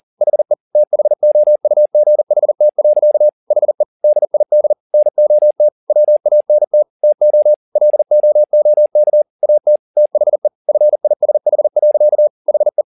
Morse code, heard as a keyed tone, about 40 wpm, 600 Hz. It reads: HE THOUGHT. HE DID NOT WANT TO LOOK AT THE FISH. HE
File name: the old man and the sea 2 40wpm